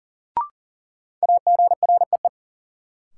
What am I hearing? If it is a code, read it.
Morse code, heard as a keyed tone, about 40 wpm, 700 Hz. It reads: AGREE
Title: Sets of 5 Words Encore - New Word Drill - 40wpm